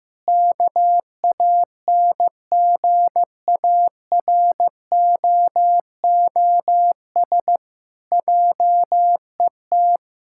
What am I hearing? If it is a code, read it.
Morse code, heard as a keyed tone, about 15 wpm, 700 Hz. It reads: KANGAROOS JET
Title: Pangram Challenge v2 - 15wpm